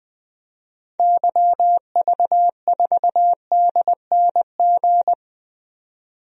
Morse code, heard as a keyed tone, about 20 wpm, 700 Hz. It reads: YV4DNG
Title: DXCC-100 Call Signs - 6 Characters - 20wpm